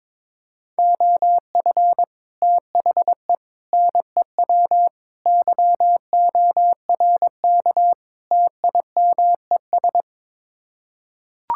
Morse code, heard as a keyed tone, about 22 wpm, 700 Hz. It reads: OF THE NEW YORK TIMES
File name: practice / 22wpm / Build A Phrase - 5-Word Phrases - 22wpm